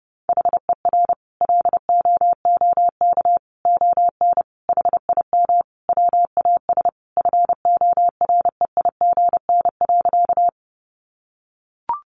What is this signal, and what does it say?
Morse code, heard as a keyed tone, about 30 wpm, 700 Hz. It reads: HER LOOK ON HIM WAS FOREIGN.
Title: Sentences from Top 1400 Words 30wpm